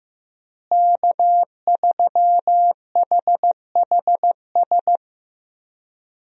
Morse code, heard as a keyed tone, about 15 wpm, 700 Hz. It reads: K3HHS
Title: USA Call Signs - 5 Characters - Repeated 3x - 15wpm